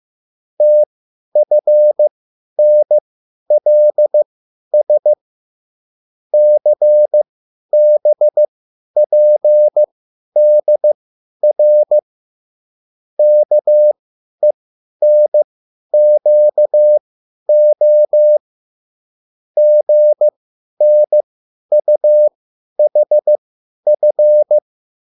Morse code, heard as a keyed tone, about 15 wpm, spaced as for 10 wpm, 600 Hz. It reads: TFNLS CBPDR KENQO GNUHF